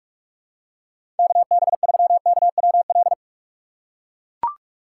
Morse code, heard as a keyed tone, about 44 wpm, 700 Hz. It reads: KB3KWL
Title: USA Call Signs - 6 Characters - 4x Character Spacing - 45wpm